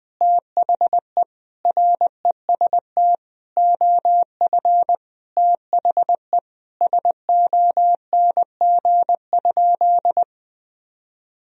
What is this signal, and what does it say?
Morse code, heard as a keyed tone, about 20 wpm, 700 Hz. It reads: THE REST OF THE SONG?
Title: Sentences from Top 1000 Words - Review Part 2 - 20wpm